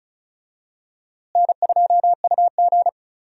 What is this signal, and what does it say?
Morse code, heard as a keyed tone, about 36 wpm, 700 Hz. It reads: N2UG
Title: USA Call Signs - 4 Characters - Repeated 2x - 35wpm